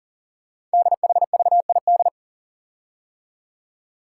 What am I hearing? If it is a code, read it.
Morse code, heard as a keyed tone, about 40 wpm, 700 Hz. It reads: DHVID
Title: Code Groups - 5 Letters - 40wpm